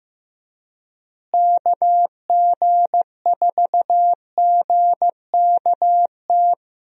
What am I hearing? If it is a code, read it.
Morse code, heard as a keyed tone, about 16 wpm, 700 Hz. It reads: KG4GKT
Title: USA Call Signs - 6 Characters - Repeated 2x - 15wpm